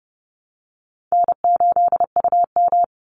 Morse code, heard as a keyed tone, about 30 wpm, 700 Hz. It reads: N8UM
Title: Call Signs Repeated 2x - 30wpm